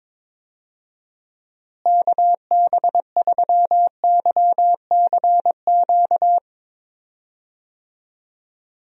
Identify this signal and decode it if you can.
Morse code, heard as a keyed tone, about 22 wpm, 700 Hz. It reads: KB3YCQ
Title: USA Call Signs - 6 Characters - Repeated 3x - 22wpm